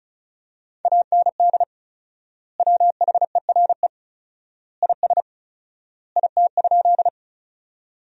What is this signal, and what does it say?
Morse code, heard as a keyed tone, about 36 wpm, 700 Hz. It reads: AND WHERE IS IT?
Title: 4x Word-Spacing - Sentences from Top 200 Words - 35wpm